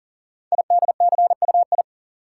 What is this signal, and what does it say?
Morse code, heard as a keyed tone, about 42 wpm, 700 Hz. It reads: IDCUI
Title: Code Groups - 5 Letter:Numbers - 40wpm